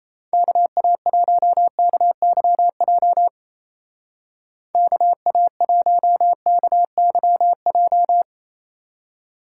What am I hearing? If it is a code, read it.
Morse code, heard as a keyed tone, about 34 wpm, 700 Hz. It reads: KA1KYJ KA1KYJ
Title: Speed Racing - USA Call Signs 28wpm